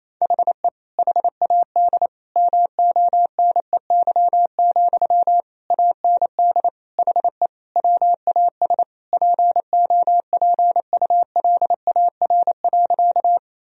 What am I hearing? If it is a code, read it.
Morse code, heard as a keyed tone, about 28 wpm, 700 Hz. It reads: HE HAD MONEY, AND HE WAS POPULAR.
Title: Sentences from Top 1000 Words - Review Part 3 - 28wpm